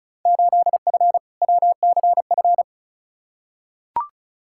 Morse code, heard as a keyed tone, about 36 wpm, 700 Hz. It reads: 8F WCF
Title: Field Day Exchanges Encore - 35wpm